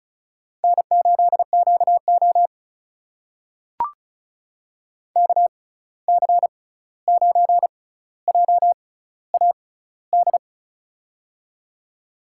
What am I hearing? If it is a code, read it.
Morse code, heard as a keyed tone, about 36 wpm, spaced as for 16 wpm, 700 Hz. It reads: N8QO KC9JAD
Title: USA Call Signs - 6x Character Spacing - 35wpm